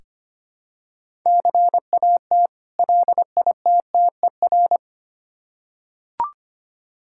Morse code, heard as a keyed tone, about 25 wpm, 700 Hz. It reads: CAT LITTER
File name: practice / 25wpm / Difficult Words with Hints - 25wpm